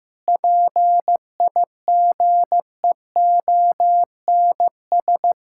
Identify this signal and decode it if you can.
Morse code, heard as a keyed tone, about 15 wpm, 700 Hz. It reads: PIGEONS
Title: Pangram Challenge v2 - 15wpm